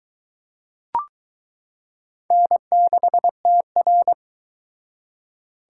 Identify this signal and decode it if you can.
Morse code, heard as a keyed tone, about 22 wpm, 700 Hz. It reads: N6TR